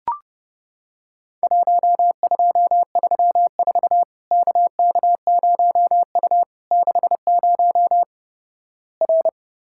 Morse code, heard as a keyed tone, about 30 wpm, 700 Hz. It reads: 1234 KK0U 60 R
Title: Worked All Europe DX Contest - QTC Exchange - 30wpm